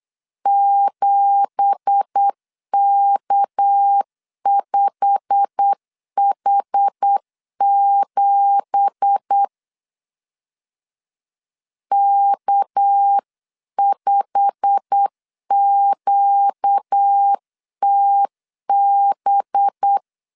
Morse code, heard as a keyed tone, about 8 wpm, 800 Hz. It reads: 7K5H7 K5QTB
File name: CW / DET / 1 9 1500C S C 8wpm 5653